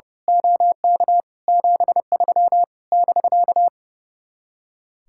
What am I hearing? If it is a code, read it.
Morse code, heard as a keyed tone, about 29 wpm, 700 Hz. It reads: OK 73 <BK>
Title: Speed-Racing Common QSO Phrases - 30wpm